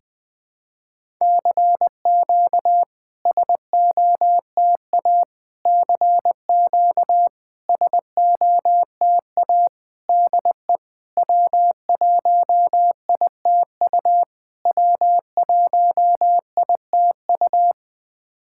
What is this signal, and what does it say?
Morse code, heard as a keyed tone, about 21 wpm, 690 Hz. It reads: CQ SOTA CQ SOTA DE W1ITU W1ITU
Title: SOTA QSO Practice 20wpm